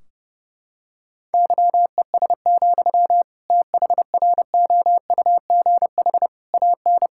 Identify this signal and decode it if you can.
Morse code, heard as a keyed tone, about 30 wpm, 700 Hz. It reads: YES, THROUGH AN
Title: Sentences from Top 1600 Words 30wpm